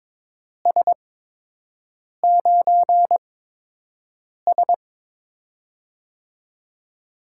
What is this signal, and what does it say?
Morse code, heard as a keyed tone, about 22 wpm, spaced as for 6 wpm, 700 Hz. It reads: S9S
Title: USA Call Signs - 4 Characters - 8x Character Spacing - 22wpm